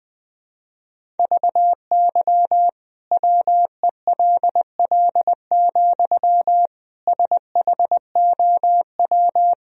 Morse code, heard as a keyed tone, about 20 wpm, 700 Hz. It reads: VY WELL, SHOW
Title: Sentences from Top 200 Words and Abbreviations 20wpm